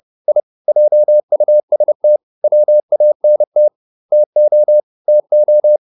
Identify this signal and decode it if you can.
Morse code, heard as a keyed tone, about 30 wpm, 600 Hz. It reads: I JUST WANT TO TO